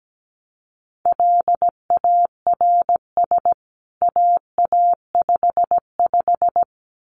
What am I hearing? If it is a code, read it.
Morse code, heard as a keyed tone, about 17 wpm, 700 Hz. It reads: LARS AA55
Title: CWT Exchange 17wpm